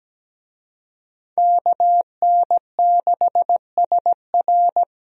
Morse code, heard as a keyed tone, about 17 wpm, 700 Hz. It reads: KN6SR
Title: USA Call Signs - 5 Characters - Repeated 2x - 17wpm